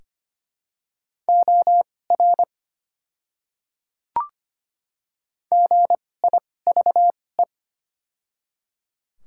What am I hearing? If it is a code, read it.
Morse code, heard as a keyed tone, about 25 wpm, spaced as for 18 wpm, 700 Hz. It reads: OR GIVE
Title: 2x Character Spacing - Top 200 Words - 25wpm